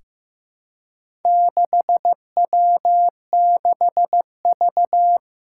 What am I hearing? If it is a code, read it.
Morse code, heard as a keyed tone, about 15 wpm, 700 Hz. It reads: 6W6V
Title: Four Challenging Letter-Number Combinations 15wpm